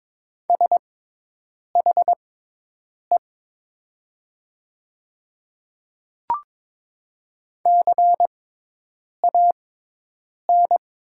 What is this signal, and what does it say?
Morse code, heard as a keyed tone, about 22 wpm, spaced as for 8 wpm, 700 Hz. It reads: SHE CAN
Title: 6x Character Spacing - Top 100 Words - 22wpm